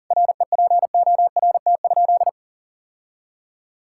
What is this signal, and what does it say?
Morse code, heard as a keyed tone, about 37 wpm, 700 Hz. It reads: REPORT?
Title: Sentences from Top 1000 Words - Review Part 6 - 40wpm